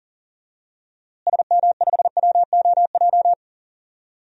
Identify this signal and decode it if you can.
Morse code, heard as a keyed tone, about 40 wpm, 700 Hz. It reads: SM5WOJ
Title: DXCC-100 Call Signs - 6 Characters - Repeated 3x - 40wpm